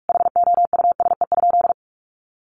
Morse code, heard as a keyed tone, about 46 wpm, 700 Hz. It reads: HOUSE?
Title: Sentences from Top 1100 Words 45wpm